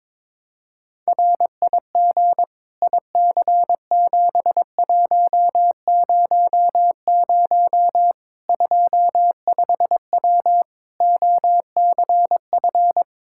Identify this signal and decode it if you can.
Morse code, heard as a keyed tone, about 22 wpm, 700 Hz. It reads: RIG IC7100 25W OCF
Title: K1USN SST QSO Practice 22wpm